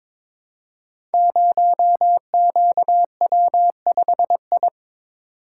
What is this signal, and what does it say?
Morse code, heard as a keyed tone, about 22 wpm, 700 Hz. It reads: 0QW5I
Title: Code Groups - 5 Letter:Numbers - 22wpm